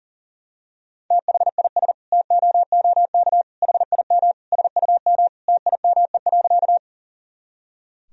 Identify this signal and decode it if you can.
Morse code, heard as a keyed tone, about 40 wpm, 700 Hz. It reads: THIS TOOK HIM SUM TIME.
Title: Sentences from Top 300 Words and Abbreviations 40wpm